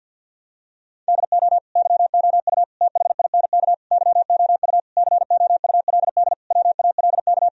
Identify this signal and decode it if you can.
Morse code, heard as a keyed tone, about 51 wpm, 700 Hz. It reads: DO YOU THINK YOU COULD WALK
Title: Sentences from Top 1000 Words - Review Part 1 - 50wpm